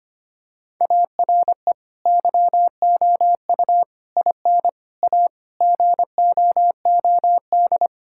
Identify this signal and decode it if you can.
Morse code, heard as a keyed tone, about 25 wpm, 700 Hz. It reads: ARE YOU IN A GOOD
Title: Sentences from Top 1000 Words - Review Part 5 - 25wpm